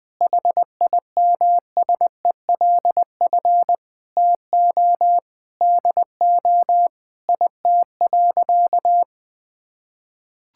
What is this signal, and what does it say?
Morse code, heard as a keyed tone, about 20 wpm, 700 Hz. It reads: HIMSELF TO DO IT.
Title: Sentences from Top 1600 Words 20wpm